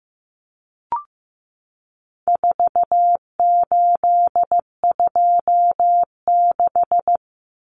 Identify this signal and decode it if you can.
Morse code, heard as a keyed tone, about 15 wpm, 700 Hz. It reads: 4826
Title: Numbers - 4-Digit 15wpm